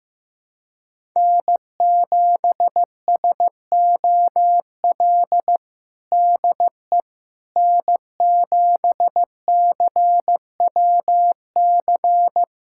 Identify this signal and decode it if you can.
Morse code, heard as a keyed tone, about 15 wpm, 700 Hz. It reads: N7SOL DE N7CWC
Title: USA Callsign DE Callsign - 5 Characters - 15wpm